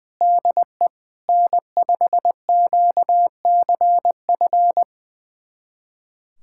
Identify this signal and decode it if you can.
Morse code, heard as a keyed tone, about 20 wpm, 700 Hz. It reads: DE N5QCF
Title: USA Callsign DE Callsign 20wpm